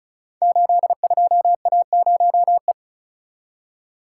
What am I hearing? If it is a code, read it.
Morse code, heard as a keyed tone, about 36 wpm, 700 Hz. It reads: 82A0E
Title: Code Groups - 5 Letter:Numbers - 35wpm